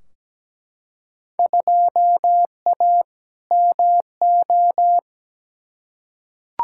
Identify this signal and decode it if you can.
Morse code, heard as a keyed tone, about 17 wpm, 700 Hz. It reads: 2A MO
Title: Field Day Exchanges Encore - 17wpm